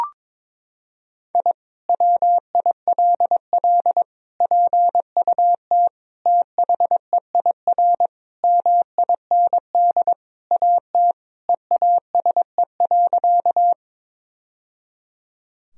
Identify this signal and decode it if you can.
Morse code, heard as a keyed tone, about 22 wpm, 700 Hz. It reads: I WILL PUT THEIR MIND AT EASE.
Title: Sentences from Top 1000 Words - Review Part 1 - 22wpm